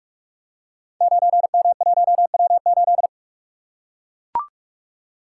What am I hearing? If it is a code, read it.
Morse code, heard as a keyed tone, about 44 wpm, 700 Hz. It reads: 9M1W8
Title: Code Groups - 5 Letter:Numbers - 45wpm